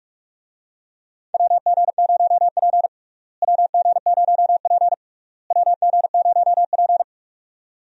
Morse code, heard as a keyed tone, about 46 wpm, 700 Hz. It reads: WG0P WG0P WG0P
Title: USA Call Signs - 4 Characters - Repeated 3x - 45wpm